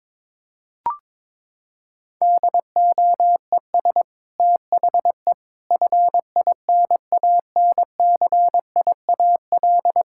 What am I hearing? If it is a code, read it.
Morse code, heard as a keyed tone, about 22 wpm, 700 Hz. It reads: DOES THE FINANCIAL
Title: Sentences from Top 1000 Words - Review Part 5 - 22wpm